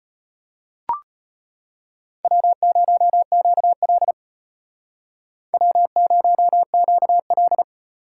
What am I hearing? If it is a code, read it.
Morse code, heard as a keyed tone, about 32 wpm, 700 Hz. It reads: W0QL W0QL